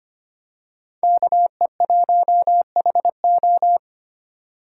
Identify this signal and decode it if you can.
Morse code, heard as a keyed tone, about 25 wpm, 700 Hz. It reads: KE1HO
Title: USA Call Signs - 5 Characters - 4x Character Spacing - 25wpm